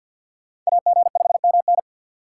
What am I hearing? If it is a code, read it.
Morse code, heard as a keyed tone, about 66 wpm, spaced as for 58 wpm, 700 Hz. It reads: AG5MN
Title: Speed Racing - USA 2x2 Call Signs 50wpm